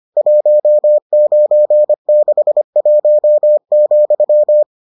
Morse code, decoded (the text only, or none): 1961,